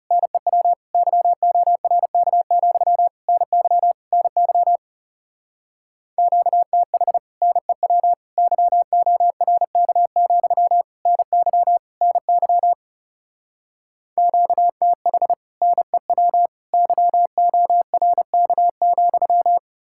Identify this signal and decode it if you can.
Morse code, heard as a keyed tone, about 38 wpm, 700 Hz. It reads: NEW YORK, NY NY QTH NEW YORK, NY NY QTH NEW YORK,